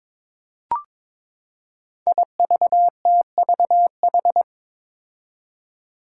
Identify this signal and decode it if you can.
Morse code, heard as a keyed tone, about 22 wpm, 700 Hz. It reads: IVTVH
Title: Five Challenging Letter-Number Combinations 22wpm